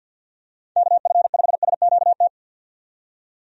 Morse code, heard as a keyed tone, about 49 wpm, 700 Hz. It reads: KF5SQT
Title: USA Call Signs - 6 Characters - 50wpm